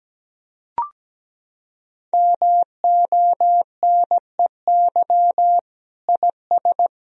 Morse code, heard as a keyed tone, about 17 wpm, 700 Hz. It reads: MONEY IS
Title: Sentences from Top 1000 Words - Review Part 2 - 17wpm